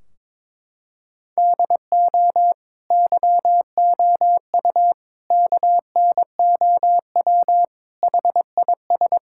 Morse code, heard as a keyed tone, about 22 wpm, 700 Hz. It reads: DO YOU KNOW HIS
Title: Sentences from Top 1600 Words 22wpm